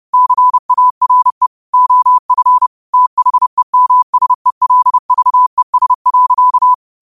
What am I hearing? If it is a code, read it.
Morse code, heard as a keyed tone, about 30 wpm, 1000 Hz. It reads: CARE OF THEMSELVES.
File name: qotdfiles / 30wpm